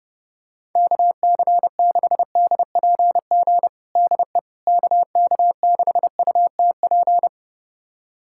Morse code, heard as a keyed tone, about 30 wpm, 700 Hz. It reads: KC6DPG DE KK6UTP
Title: USA Callsign DE Callsign - 6 Characters - 30wpm